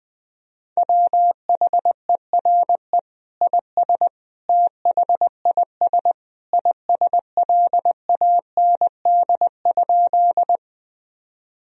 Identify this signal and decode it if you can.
Morse code, heard as a keyed tone, about 21 wpm, 700 Hz. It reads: WHERE IS THIS ISLAND?
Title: Sentences from Top 1000 Words - Review Part 3 - 20wpm